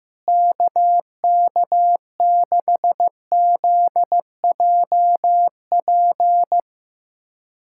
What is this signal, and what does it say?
Morse code, heard as a keyed tone, about 15 wpm, 700 Hz. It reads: KK6ZJP
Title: USA Callsign DE Callsign - 6 Characters - 15wpm